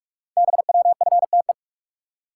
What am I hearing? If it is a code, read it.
Morse code, heard as a keyed tone, about 44 wpm, 690 Hz. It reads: BWFTE
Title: Code Groups - 5 Letters - 45wpm